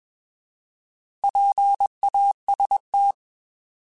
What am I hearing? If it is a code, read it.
Morse code, heard as a keyed tone, about 21 wpm, 800 Hz. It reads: PAST
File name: Alphabetical / Alphabetical mp3s / Ps @20